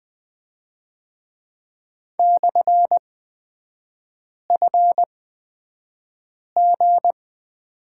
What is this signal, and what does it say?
Morse code, heard as a keyed tone, about 20 wpm, spaced as for 5 wpm, 700 Hz. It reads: /FG
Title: 224 Lesson 36e - BT - Three Characters 20wpm